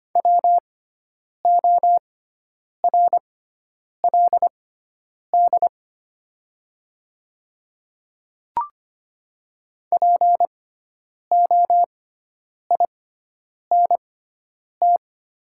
Morse code, heard as a keyed tone, about 25 wpm, spaced as for 9 wpm, 690 Hz. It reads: WORLD POINT
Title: 6x Character Spacing - Top 200 Words - 25wpm